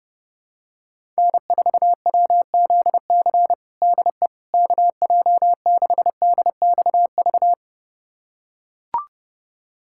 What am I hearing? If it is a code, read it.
Morse code, heard as a keyed tone, about 30 wpm, 700 Hz. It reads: N4WZC DE KJ6DXV
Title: USA Callsign DE Callsign 30wpm